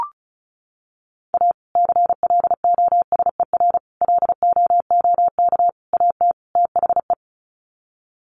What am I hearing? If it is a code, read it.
Morse code, heard as a keyed tone, about 34 wpm, 700 Hz. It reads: A CLOSER LOOK AT THE
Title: Sets of 5 Words Encore 35wpm